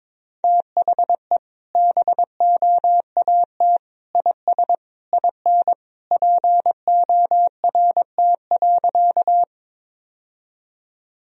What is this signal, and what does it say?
Morse code, heard as a keyed tone, about 22 wpm, 700 Hz. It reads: THE BOAT IS IN PORT.
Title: Sentences from Top 1000 Words - Review Part 1 - 22wpm